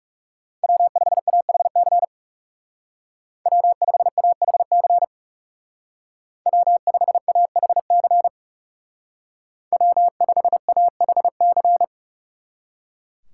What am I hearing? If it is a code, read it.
Morse code, heard as a keyed tone, about 36 wpm, 700 Hz. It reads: W5AHC W5AHC W5AHC W5AHC
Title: Speed Racing - USA 1x3 Call Signs 30wpm